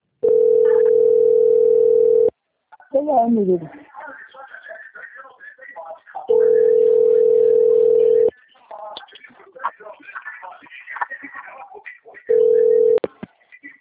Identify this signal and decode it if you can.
Ringback tone (North America), 440 + 480 Hz, with speech.
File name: out- - - - - .23667